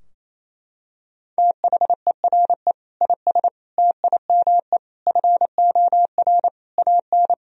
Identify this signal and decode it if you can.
Morse code, heard as a keyed tone, about 28 wpm, 700 Hz. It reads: THERE IS TIME FOR AN
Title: Sentences from Top 1000 Words - Review Part 3 - 28wpm